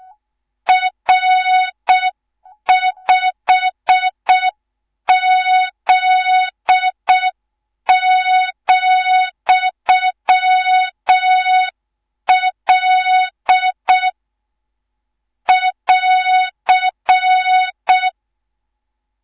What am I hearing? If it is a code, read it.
Morse code, heard as a keyed tone, about 6 wpm, 3700 Hz. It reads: R5Z,L <AR>
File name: Morze-HA2MN / hang / 18 IRASJELEK2